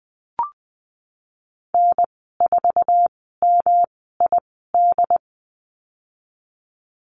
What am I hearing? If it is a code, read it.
Morse code, heard as a keyed tone, about 19 wpm, spaced as for 14 wpm, 700 Hz. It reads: N4MID
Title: 147 Lesson 24g - Letter y - Call Signs 20wpm